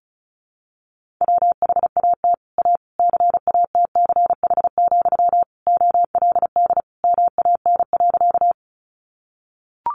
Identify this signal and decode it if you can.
Morse code, heard as a keyed tone, about 34 wpm, 700 Hz. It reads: WHAT A CATCH, OLD MAN.
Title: Sentences from Top 1500 Words 35wpm